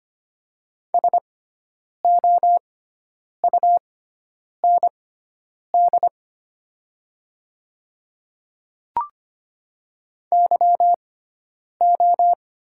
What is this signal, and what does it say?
Morse code, heard as a keyed tone, about 25 wpm, spaced as for 9 wpm, 700 Hz. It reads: SOUND YO
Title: 6x Character Spacing - Top 100 Words - 25wpm